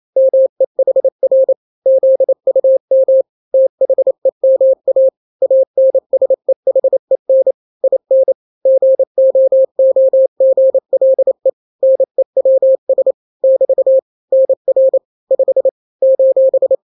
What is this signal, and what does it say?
Morse code, heard as a keyed tone, about 28 wpm, 530 Hz. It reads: MEHR ZUM THEMA ANSEHEN IN GOOGLE NEWS <BT> NR 5 :
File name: daily / 27wpm